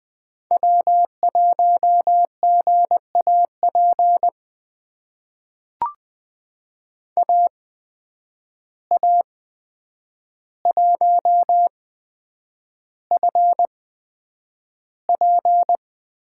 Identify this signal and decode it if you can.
Morse code, heard as a keyed tone, about 20 wpm, spaced as for 9 wpm, 700 Hz. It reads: W1GAP AA1FP
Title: USA Call Signs - 5 Characters - 8x Character Spacing - 20wpm